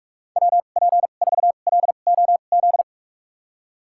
Morse code, heard as a keyed tone, about 45 wpm, spaced as for 32 wpm, 700 Hz. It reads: WP4LOZ
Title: USA Call Signs - 6 Characters - 2x Character Spacing - 45wpm